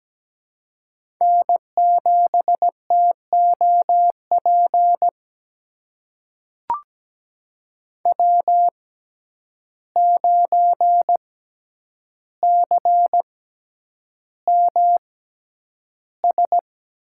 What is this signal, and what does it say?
Morse code, heard as a keyed tone, about 17 wpm, spaced as for 9 wpm, 700 Hz. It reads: N7TOP W9CMS
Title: USA Call Signs - 5 Characters - 6x Character Spacing - 17wpm